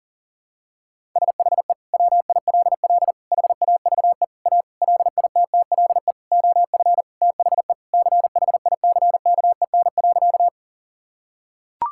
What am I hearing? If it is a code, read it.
Morse code, heard as a keyed tone, about 40 wpm, 700 Hz. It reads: SHE WILL HAVE A LITTLE OF THE CHICKEN.
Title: Sentences from Top 1000 Words - Review Part 5 - 40wpm